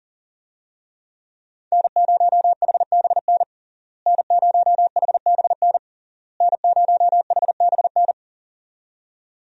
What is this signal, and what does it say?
Morse code, heard as a keyed tone, about 40 wpm, 700 Hz. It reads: N0HBN N0HBN N0HBN
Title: USA Call Signs - 5 Characters - Repeated 3x - 40wpm